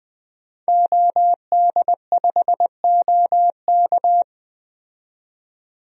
Morse code, heard as a keyed tone, about 20 wpm, 700 Hz. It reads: OD5OK
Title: DXCC-100 Call Signs - 5 Characters - 20wpm